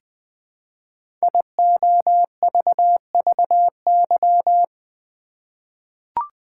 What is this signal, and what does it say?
Morse code, heard as a keyed tone, about 20 wpm, 700 Hz. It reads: IOVVY
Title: Code Groups - 5 Letter:Numbers - 20wpm